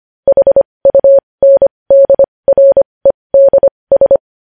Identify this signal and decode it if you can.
Morse code, heard as a keyed tone, about 25 wpm, spaced as for 20 wpm, 550 Hz. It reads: HUNDREDS
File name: podcast / News headlines-1x-25x20